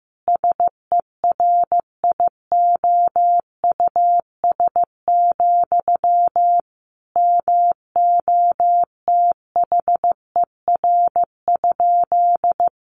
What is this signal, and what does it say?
Morse code, heard as a keyed tone, about 15 wpm, 700 Hz. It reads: SERIOUS, MOTHER?
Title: Sentences from Top 1100 Words 15wpm